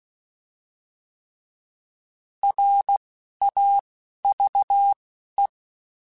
Morse code, heard as a keyed tone, about 16 wpm, 800 Hz. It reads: R A V E